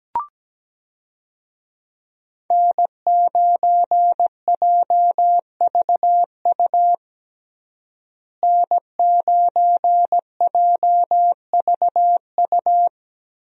Morse code, heard as a keyed tone, about 17 wpm, 700 Hz. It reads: N9JVU N9JVU
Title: USA Call Signs - 5 Characters - Repeated 2x - 17wpm